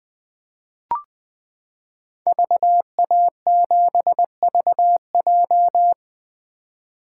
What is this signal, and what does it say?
Morse code, heard as a keyed tone, about 20 wpm, 700 Hz. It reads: VA7VJ
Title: 2023 SOTA Chaser Call Signs - 20wpm